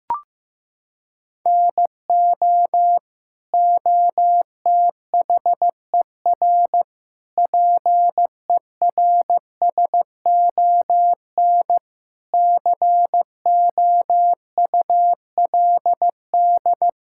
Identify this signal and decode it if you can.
Morse code, heard as a keyed tone, about 15 wpm, 700 Hz. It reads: NO OTHER PERSON COULD
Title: Sentences from Top 1000 Words - Review Part 5 - 15wpm